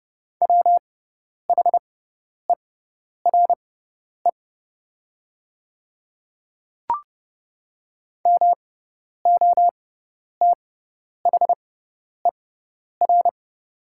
Morse code, heard as a keyed tone, about 30 wpm, spaced as for 10 wpm, 690 Hz. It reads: WHERE MOTHER
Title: 6x Character Spacing - Top 200 Words - 30wpm